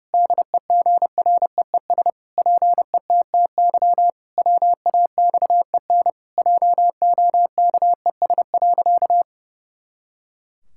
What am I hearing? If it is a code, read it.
Morse code, heard as a keyed tone, about 30 wpm, 700 Hz. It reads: DEGREES PETTY WAXEN JOKES.
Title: Pangram Challenge v2 - 30wpm